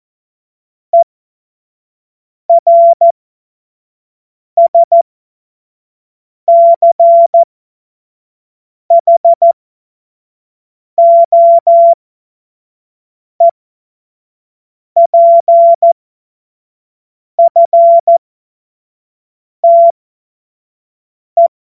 Morse code, heard as a keyed tone, about 14 wpm, spaced as for 5 wpm, 680 Hz. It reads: ERSCHOEPFTE
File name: Morsekurs / Witze / Witz 27. Dezember 2022 25 F70